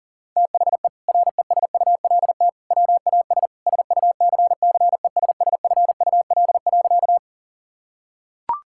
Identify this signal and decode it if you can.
Morse code, heard as a keyed tone, about 40 wpm, 700 Hz. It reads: THE RESULT WAS SUCCESSFUL.
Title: Sentences from Top 1000 Words - Review Part 6 - 40wpm